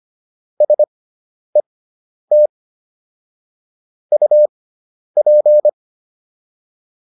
Morse code, heard as a keyed tone, about 25 wpm, spaced as for 10 wpm, 600 Hz. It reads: SET UP